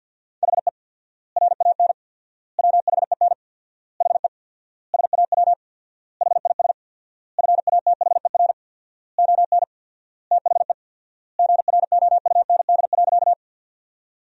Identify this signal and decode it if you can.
Morse code, heard as a keyed tone, about 50 wpm, 700 Hz. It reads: HE RAN WHEN HE SAW HIS FATHER ON THE GROUND.
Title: 4x Word-Spacing - Sentences from Top 500 Words - 50wpm